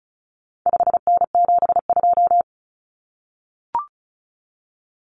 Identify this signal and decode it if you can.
Morse code, heard as a keyed tone, about 34 wpm, 700 Hz. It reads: 5N72
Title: Common Cut Numbers - 4-Digit 35wpm